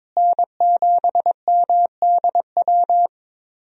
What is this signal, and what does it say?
Morse code, heard as a keyed tone, about 22 wpm, 700 Hz. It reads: N7MDW
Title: USA Call Signs - 5 Characters - Repeated 2x - 22wpm